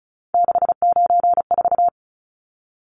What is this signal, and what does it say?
Morse code, heard as a keyed tone, about 34 wpm, 700 Hz. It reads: 694
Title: Numbers - 3-Digit 35wpm